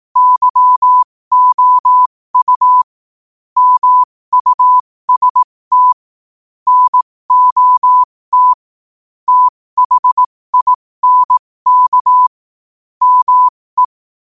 Morse code, heard as a keyed tone, about 18 wpm, spaced as for 15 wpm, 1000 Hz. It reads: YOU MUST NOT THINK ME